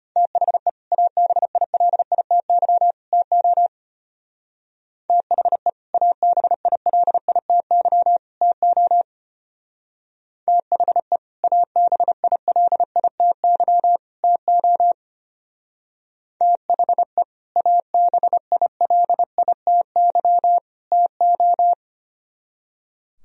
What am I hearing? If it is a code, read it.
Morse code, heard as a keyed tone, about 31 wpm, 700 Hz. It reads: THE ABILITY TO THE ABILITY TO THE ABILITY TO THE ABILITY TO
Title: Speed-Racing - Sets of 3 Words 25wpm